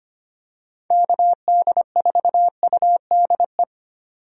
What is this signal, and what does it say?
Morse code, heard as a keyed tone, about 25 wpm, 700 Hz. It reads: KD4UDE
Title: USA Call Signs - 6 Characters - 25wpm